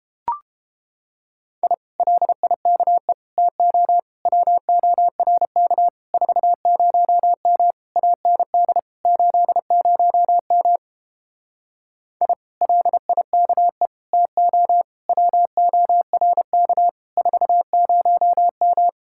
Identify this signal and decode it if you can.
Morse code, heard as a keyed tone, about 29 wpm, 700 Hz. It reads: I LIKE TO WORK 40M AND 80M I LIKE TO WORK 40M